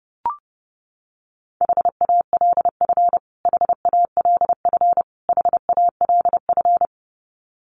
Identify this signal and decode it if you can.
Morse code, heard as a keyed tone, about 30 wpm, 700 Hz. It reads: HALF HALF HALF
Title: Top 500 Words Repeated 30wpm